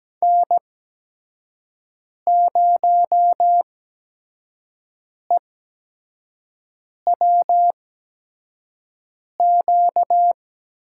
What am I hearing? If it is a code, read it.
Morse code, heard as a keyed tone, about 17 wpm, spaced as for 5 wpm, 700 Hz. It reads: N0EWQ
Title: USA Call Signs - 5 Characters - 8x Character Spacing - 17wpm